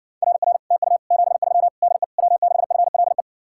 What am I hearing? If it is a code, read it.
Morse code, heard as a keyed tone, about 67 wpm, 700 Hz. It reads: RR TU 73 DE W6FLE